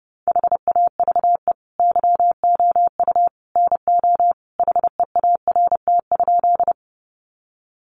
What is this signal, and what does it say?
Morse code, heard as a keyed tone, about 30 wpm, 700 Hz. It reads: HAVE YOU NO HEART?
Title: Sentences from Top 1200 Words 30wpm